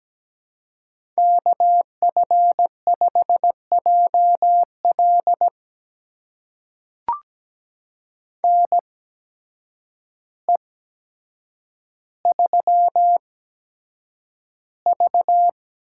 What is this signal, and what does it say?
Morse code, heard as a keyed tone, about 17 wpm, spaced as for 8 wpm, 700 Hz. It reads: KF5JL NE3V
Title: USA Call Signs - 8x Character Spacing - 17wpm